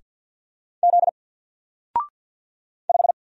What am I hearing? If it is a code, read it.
Morse code, heard as a keyed tone, about 49 wpm, 700 Hz. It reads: Z 5
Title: Single Character (Letter:Number:Punctuation:Prosign) Alt 50wpm